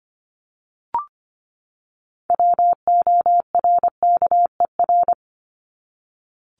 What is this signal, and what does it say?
Morse code, heard as a keyed tone, about 25 wpm, 700 Hz. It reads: WORKER
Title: Top 1751-2000 Words 25wpm